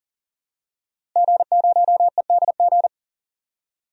Morse code, heard as a keyed tone, about 40 wpm, 700 Hz. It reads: G0EDG
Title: DXCC-100 Call Signs - 5 Characters - Repeated 2x - 40wpm